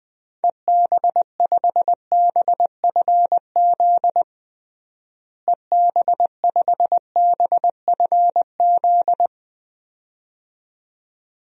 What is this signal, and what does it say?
Morse code, heard as a keyed tone, about 19 wpm, 700 Hz. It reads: EB5BFZ EB5BFZ
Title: DXCC-100 Call Signs - 6 Characters - Repeated 2x - 20wpm